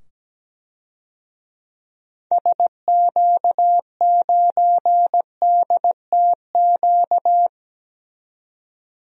Morse code, heard as a keyed tone, about 17 wpm, 700 Hz. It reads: SQ9DTQ
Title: DXCC-100 Call Signs - 6 Characters - 17wpm